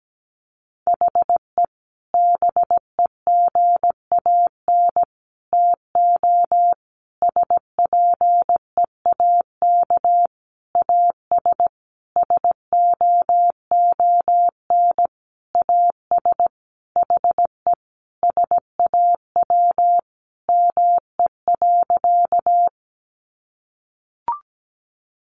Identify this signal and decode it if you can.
Morse code, heard as a keyed tone, about 17 wpm, 700 Hz. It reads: HE BEGAN TO SPEAK AS SOON AS HE SAW ME.
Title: Sentences from Top 1100 Words 17wpm